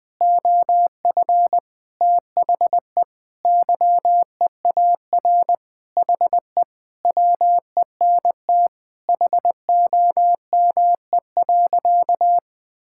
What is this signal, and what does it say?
Morse code, heard as a keyed tone, about 20 wpm, 700 Hz. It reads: OF THE YEAR HE WENT HOME.
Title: Sentences with 4-Word Phrases - 20wpm